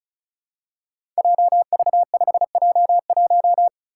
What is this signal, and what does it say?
Morse code, heard as a keyed tone, about 36 wpm, 700 Hz. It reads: JV5J1
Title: Five Challenging Letter-Number Combinations 35wpm